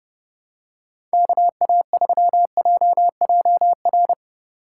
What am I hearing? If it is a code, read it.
Morse code, heard as a keyed tone, about 32 wpm, 700 Hz. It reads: KA3JJR
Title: USA Call Signs - 6 Characters - 2x Character Spacing - 30wpm